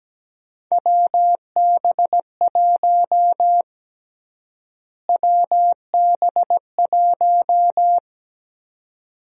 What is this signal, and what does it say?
Morse code, heard as a keyed tone, about 17 wpm, 700 Hz. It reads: WB1 WB1
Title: USA Call Sign Prefixes - Repeated 2x - 17wpm